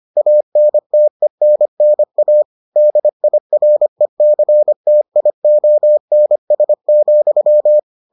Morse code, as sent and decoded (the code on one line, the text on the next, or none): .- -. - . -. -. .- / -.. .. .-. -.-. - .. --- -. ... --..--
ANTENNA DIRECTIONS,